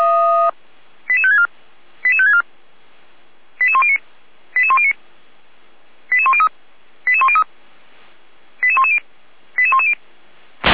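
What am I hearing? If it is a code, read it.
Five-tone selective call (ZVEI), heard as quick runs of five tones, tones of about 70 ms. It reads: 88564 88564 88109 88109 88103 88103 88110 88110
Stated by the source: Received on an SDR, ZVEI five-tone selective call, from the Feuerwehr Orlamünde (fire brigade) audio archive.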